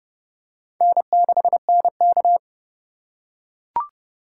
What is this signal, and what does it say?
Morse code, heard as a keyed tone, about 30 wpm, 700 Hz. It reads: N6NK
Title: USA Call Signs - 4 Characters - 4x Character Spacing - 30wpm